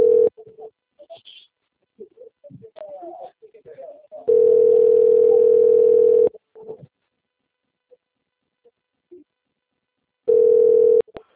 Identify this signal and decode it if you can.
Ringback tone (North America), 440 + 480 Hz, with speech.
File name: out- - - - - .27905